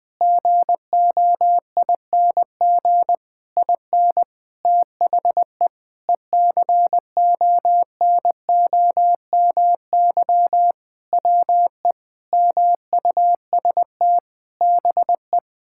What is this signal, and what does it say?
Morse code, heard as a keyed tone, about 20 wpm, 700 Hz. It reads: GOING IN THE ECONOMY WE MUST BE
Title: Sentences with 8-Word Phrases - Spoken Beginning - 20wpm